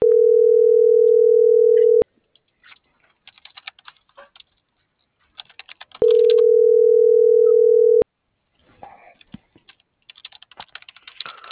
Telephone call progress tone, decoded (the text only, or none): ringback tone